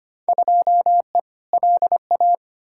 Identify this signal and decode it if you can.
Morse code, heard as a keyed tone, about 25 wpm, 700 Hz. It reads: 2E LA
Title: Field Day Exchanges Encore - 25wpm